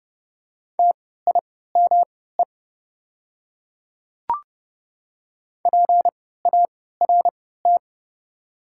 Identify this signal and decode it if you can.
Morse code, heard as a keyed tone, about 30 wpm, spaced as for 17 wpm, 690 Hz. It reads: TIME PART
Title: 3x Character Spacing - Top 500 Words - 30wpm